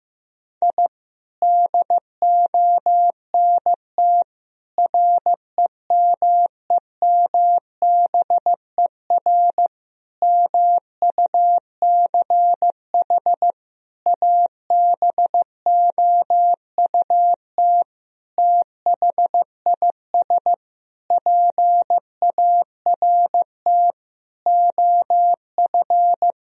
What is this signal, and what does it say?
Morse code, heard as a keyed tone, about 16 wpm, 700 Hz. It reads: I DONT REMEMBER MUCH ABOUT THIS PART OF
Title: Sentences from Top 1000 Words - Review Part 1 - 15wpm